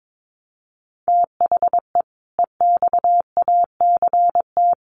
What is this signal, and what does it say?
Morse code, heard as a keyed tone, about 22 wpm, 700 Hz. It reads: THE EXACT
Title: Sentences from Top 1300 Words 22wpm